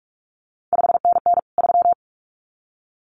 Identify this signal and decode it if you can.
Morse code, heard as a keyed tone, about 46 wpm, 700 Hz. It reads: 5NN 3
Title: CQ WW DX Contest Exchange - 45wpm